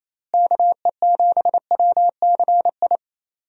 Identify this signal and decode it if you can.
Morse code, heard as a keyed tone, about 28 wpm, 700 Hz. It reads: KE7WCI
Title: USA Call Signs - 6 Characters - Repeated 3x - 28wpm